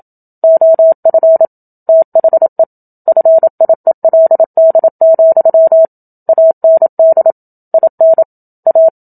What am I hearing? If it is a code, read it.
Morse code, heard as a keyed tone, about 27 wpm, 650 Hz. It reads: OF THE FIELD, AND IN A
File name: Book Two / 01-Chapter 1 - 27 WPM